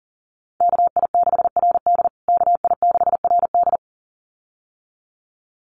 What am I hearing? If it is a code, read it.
Morse code, heard as a keyed tone, about 40 wpm, 700 Hz. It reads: KI6RD KI6RD
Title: USA Call Signs Repeated 2x 40wpm